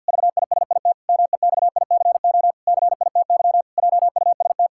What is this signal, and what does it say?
Morse code, heard as a keyed tone, about 50 wpm, 700 Hz. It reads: VISIT MEXICO CITY JUST
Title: Pangram Challenge v2 - 50wpm